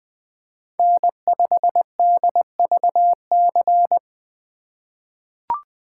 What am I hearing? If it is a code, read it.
Morse code, heard as a keyed tone, about 20 wpm, 700 Hz. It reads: N5DVC
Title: USA Call Signs - 5 Characters - 6x Character Spacing - 20wpm